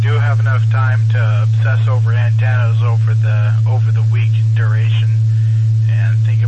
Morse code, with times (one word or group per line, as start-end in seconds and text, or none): none